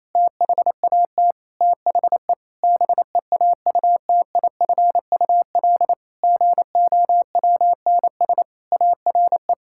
Morse code, heard as a keyed tone, about 28 wpm, 700 Hz. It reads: THAT THE BEAUTIFUL GOWNS ARE
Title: Pangram Challenge v2 - 28wpm